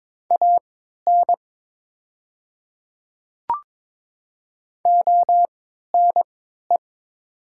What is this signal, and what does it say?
Morse code, heard as a keyed tone, about 22 wpm, spaced as for 13 wpm, 700 Hz. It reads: AN ONE
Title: 3x Character Spacing - Top 100 Words - 22wpm